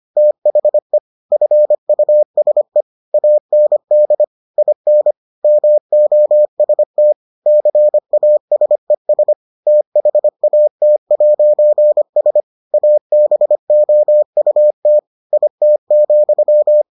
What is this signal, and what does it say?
Morse code, heard as a keyed tone, about 25 wpm, 600 Hz. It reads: THE FUSE AND IN MOST CASES THAT'S ABOUT IT,